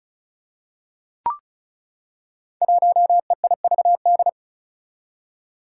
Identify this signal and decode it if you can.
Morse code, heard as a keyed tone, about 34 wpm, 700 Hz. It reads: 1EIVD
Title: Five Challenging Letter-Number Combinations 35wpm